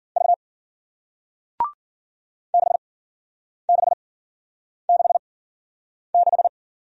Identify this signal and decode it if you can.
Morse code, heard as a keyed tone, about 47 wpm, spaced as for 9 wpm, 700 Hz. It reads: V 6666
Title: Speed-Racing Single Letter-Number 40wpm